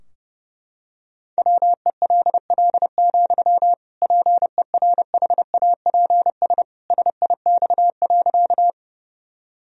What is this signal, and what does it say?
Morse code, heard as a keyed tone, about 32 wpm, 700 Hz. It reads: WELL, PERHAPS SIX.
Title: Sentences from Top 1000 Words - Review Part 2 - 30wpm